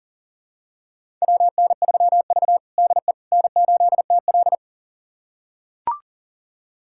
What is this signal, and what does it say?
Morse code, heard as a keyed tone, about 40 wpm, 700 Hz. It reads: WN3V DE N8TL
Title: USA Callsign DE Callsign - 4 Characters - 40wpm